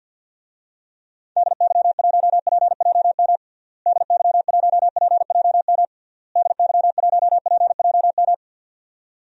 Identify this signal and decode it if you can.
Morse code, heard as a keyed tone, about 51 wpm, 700 Hz. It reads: DY1PJM DY1PJM DY1PJM
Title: DXCC-100 Call Signs - 6 Characters - Repeated 3x - 50wpm